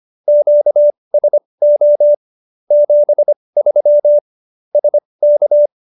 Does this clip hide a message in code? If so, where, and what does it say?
Yes: Morse code from 0.3 to 5.7 s, heard as a keyed tone, about 25 wpm, spaced as for 20 wpm, 590 Hz. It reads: QSO 73 SK